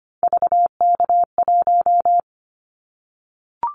Morse code, heard as a keyed tone, about 25 wpm, 700 Hz. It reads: VK1
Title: Callsign Prefixes for Top 100 DXCC - 25wpm